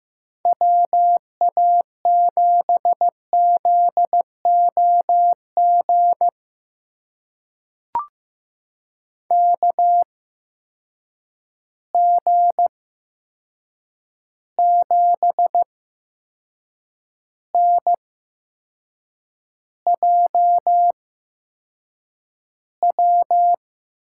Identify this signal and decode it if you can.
Morse code, heard as a keyed tone, about 15 wpm, spaced as for 6 wpm, 700 Hz. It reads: WA7ZOG KG7NJW
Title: USA Call Signs - 6 Characters - 8x Character Spacing - 15wpm